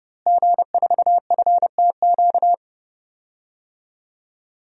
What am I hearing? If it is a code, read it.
Morse code, heard as a keyed tone, about 32 wpm, 700 Hz. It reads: G4FTQ
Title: DXCC-100 Call Signs - 5 Characters - Repeated 2x - 30wpm